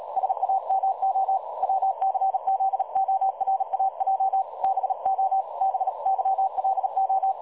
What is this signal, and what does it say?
Morse code, heard as a keyed tone, about 37 wpm, 770 Hz. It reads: SAMO WCZORAJ BO BYLO
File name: dlazaawansowanych / nagraniazpasma / websdr recording 3565.0kHz